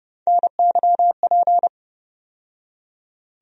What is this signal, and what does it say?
Morse code, heard as a keyed tone, about 30 wpm, 700 Hz. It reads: NYP
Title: USA Call Sign Postfixes - 3 Letters 30wpm